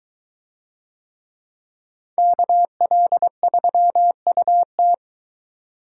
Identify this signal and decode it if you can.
Morse code, heard as a keyed tone, about 22 wpm, 700 Hz. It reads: KL3UT